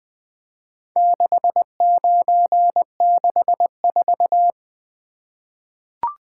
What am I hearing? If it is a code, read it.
Morse code, heard as a keyed tone, about 20 wpm, 700 Hz. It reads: 6964
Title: Four Letter-Number Combinations - 20wpm